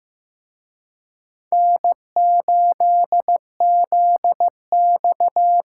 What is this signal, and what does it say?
Morse code, heard as a keyed tone, about 18 wpm, 700 Hz. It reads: N8ZX
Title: Speed Racing - USA 1x2 Call Signs 15wpm